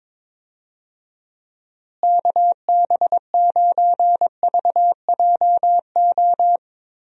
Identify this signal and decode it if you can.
Morse code, heard as a keyed tone, about 22 wpm, 700 Hz. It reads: KB9VJO
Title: USA Call Signs - 6 Characters - Repeated 3x - 22wpm